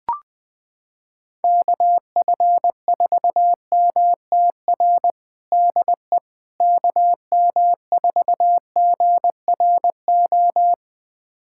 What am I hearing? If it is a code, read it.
Morse code, heard as a keyed tone, about 20 wpm, 700 Hz. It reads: KF4MTR DE KM4GRO
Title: USA Callsign DE Callsign 20wpm